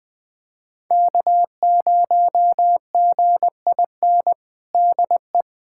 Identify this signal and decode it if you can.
Morse code, heard as a keyed tone, about 20 wpm, 700 Hz. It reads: K0GIN DE
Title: USA Callsign DE Callsign - 5 Characters - 20wpm